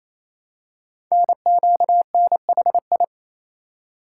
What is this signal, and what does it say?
Morse code, heard as a keyed tone, about 27 wpm, 700 Hz. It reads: NQNHI
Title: Code Groups - 5 Letters - 28wpm